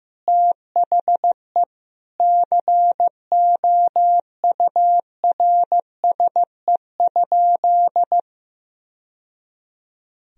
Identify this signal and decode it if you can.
Morse code, heard as a keyed tone, about 15 wpm, 700 Hz. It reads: THE COURSE?
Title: Sentences from Top 1000 Words - Review Part 2 - 15wpm